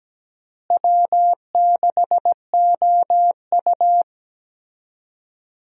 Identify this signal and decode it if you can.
Morse code, heard as a keyed tone, about 17 wpm, 700 Hz. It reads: W6OU
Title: USA Call Signs - 4 Characters - Repeated 3x - 17wpm